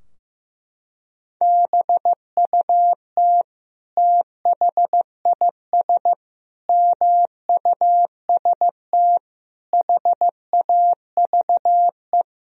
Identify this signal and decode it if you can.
Morse code, heard as a keyed tone, about 16 wpm, 700 Hz. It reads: BUT THIS MUST HAVE
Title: Sentences from Top 1000 Words - Review Part 3 - 15wpm